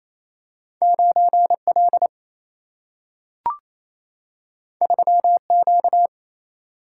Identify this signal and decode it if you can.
Morse code, heard as a keyed tone, about 28 wpm, 700 Hz. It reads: 9L 3Q
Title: Single-Letter Single-Number 28wpm